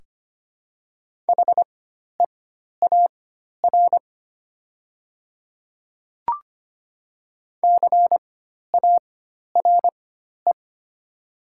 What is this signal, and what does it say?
Morse code, heard as a keyed tone, about 25 wpm, spaced as for 12 wpm, 700 Hz. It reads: HEAR CARE
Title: 4x Character Spacing - Top 500 Words - 25wpm